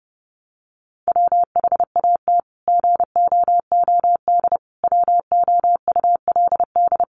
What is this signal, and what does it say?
Morse code, heard as a keyed tone, about 30 wpm, 700 Hz. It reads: WHAT GOOD WOULD
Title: Sentences from Top 1200 Words 30wpm